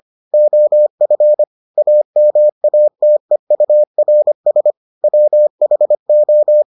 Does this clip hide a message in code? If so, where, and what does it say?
Yes: Morse code from 0.3 to 6.6 s, heard as a keyed tone, about 25 wpm, 600 Hz. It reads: OF AMATEURS WHO